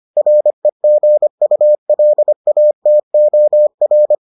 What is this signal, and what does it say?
Morse code, heard as a keyed tone, about 25 wpm, 600 Hz. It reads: REGULATOR